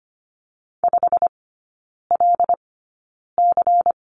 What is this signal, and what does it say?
Morse code, heard as a keyed tone, about 25 wpm, spaced as for 9 wpm, 700 Hz. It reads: 5LC
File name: course / 25:10wpm / 093 Lesson 16e - Letter c - Three Characters 25wpm